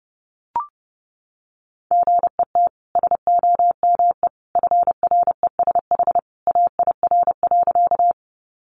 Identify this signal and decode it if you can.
Morse code, heard as a keyed tone, about 30 wpm, 700 Hz. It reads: GET SOME FRESH AIR.
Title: Sentences from Top 1200 Words 30wpm